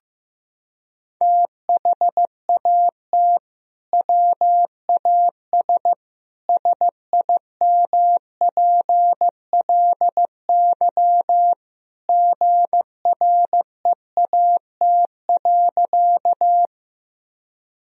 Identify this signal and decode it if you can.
Morse code, heard as a keyed tone, about 15 wpm, 700 Hz. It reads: THAT WAS SIMPLY GREAT.
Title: Sentences from Top 1000 Words - Review Part 4 - 15wpm